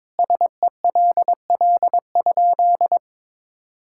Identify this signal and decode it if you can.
Morse code, heard as a keyed tone, about 22 wpm, 700 Hz. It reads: SELL?
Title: Sentences from Top 1600 Words 22wpm